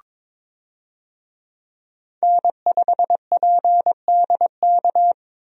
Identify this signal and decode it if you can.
Morse code, heard as a keyed tone, about 22 wpm, 700 Hz. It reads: N5PDK
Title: USA Call Signs - 5 Characters - 22wpm